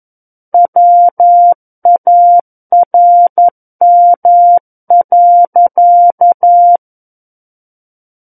Morse code, heard as a keyed tone, about 11 wpm, 700 Hz. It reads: WARM.
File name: Book One / 02-Chapter 2 - 11 WPM